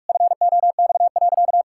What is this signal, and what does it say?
Morse code, heard as a keyed tone, about 46 wpm, 700 Hz. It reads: FOX.
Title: Pangram Challenge v2 - 45wpm